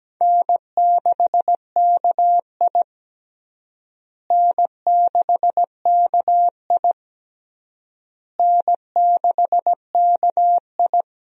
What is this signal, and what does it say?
Morse code, heard as a keyed tone, about 17 wpm, 700 Hz. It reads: N6KI N6KI N6KI
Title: USA Call Signs - 4 Characters - Repeated 3x - 17wpm